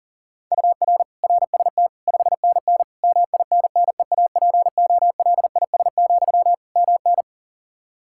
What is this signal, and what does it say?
Morse code, heard as a keyed tone, about 40 wpm, 700 Hz. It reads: UR RST 5NN MINNEAPOLIS, MN
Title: Practice QSO - Exchange 1 - Beginning - 40wpm